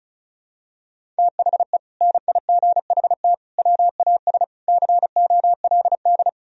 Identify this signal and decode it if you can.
Morse code, heard as a keyed tone, about 36 wpm, 700 Hz. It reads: THE NIGHT WAS COLD
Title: Sentences from Top 400 Words and Abbreviations 35wpm